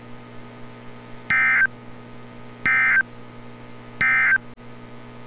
Emergency Alert System (SAME) tones, heard as harsh buzzing bursts. It is heard from 1.2 to 4.5 s.
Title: NOAA Weather Radio - Altoona, PA